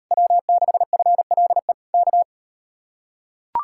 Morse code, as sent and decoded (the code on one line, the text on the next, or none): .-- -.... ..-. .-.. / -.-
W6FLE K